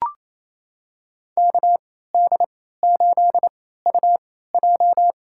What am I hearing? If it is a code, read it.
Morse code, heard as a keyed tone, about 28 wpm, spaced as for 16 wpm, 700 Hz. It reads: KD8UJ
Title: USA Call Signs - 3x Character Spacing - 28wpm